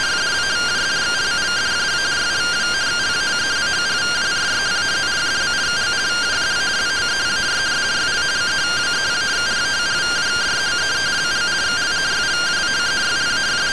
Two-tone teleprinter signal, about 170 Hz shift.
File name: findings / MR CORONAL 55.gen-12 rtty rx